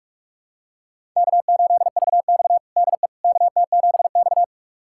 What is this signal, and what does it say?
Morse code, heard as a keyed tone, about 46 wpm, 700 Hz. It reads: K8VX DE KT7X
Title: USA Callsign DE Callsign - 4 Characters - 45wpm